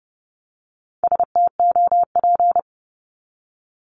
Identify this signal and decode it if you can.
Morse code, heard as a keyed tone, about 30 wpm, 700 Hz. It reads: STOP
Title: Top 500 Words 30wpm